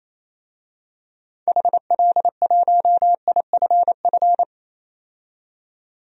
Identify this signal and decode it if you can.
Morse code, heard as a keyed tone, about 28 wpm, 700 Hz. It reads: HL1IFF
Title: DXCC-100 Call Signs - 6 Characters - 28wpm